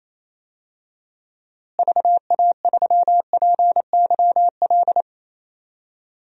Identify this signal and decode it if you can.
Morse code, heard as a keyed tone, about 28 wpm, 700 Hz. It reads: VA3PYL
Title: DXCC-100 Call Signs - 6 Characters - Repeated 3x - 28wpm